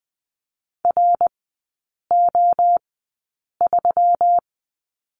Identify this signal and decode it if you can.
Morse code, heard as a keyed tone, about 20 wpm, spaced as for 8 wpm, 700 Hz. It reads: RO3
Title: 138 Lesson 23e - Letter f - Three Characters 20wpm